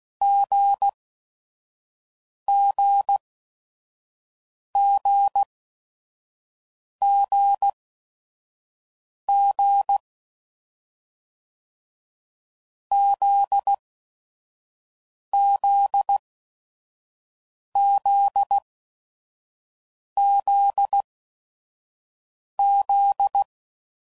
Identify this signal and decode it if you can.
Morse code, heard as a keyed tone, about 16 wpm, spaced as for 5 wpm, 800 Hz. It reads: GGGGG ZZZZZ
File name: Combinations / Difficult Combos / GZ GZ7